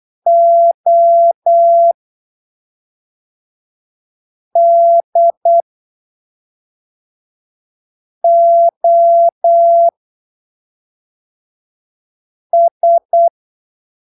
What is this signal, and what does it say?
Morse code, heard as a keyed tone, about 8 wpm, spaced as for 3 wpm, 680 Hz. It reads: ODOS
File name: arquivos / texto / texto 08 08ppm